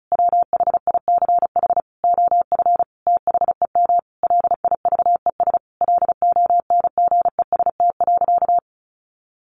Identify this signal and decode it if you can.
Morse code, heard as a keyed tone, about 34 wpm, 700 Hz. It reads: WHICH OF THEM LIVES LONGEST.
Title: A Sound Of Thunder 35wpm